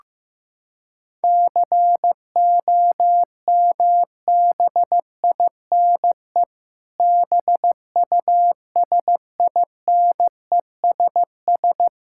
Morse code, heard as a keyed tone, about 15 wpm, 700 Hz. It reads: COMBINE BUSINESS
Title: Sentences from Top 1600 Words 15wpm